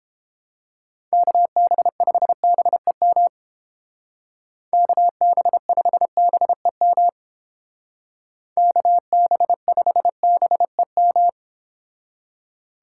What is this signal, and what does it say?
Morse code, heard as a keyed tone, about 29 wpm, 700 Hz. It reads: KB5BEM KB5BEM KB5BEM